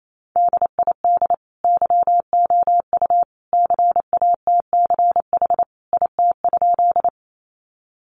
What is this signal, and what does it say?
Morse code, heard as a keyed tone, about 28 wpm, 700 Hz. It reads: DID YOU CATCH IT?
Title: Sentences from Top 1500 Words 28wpm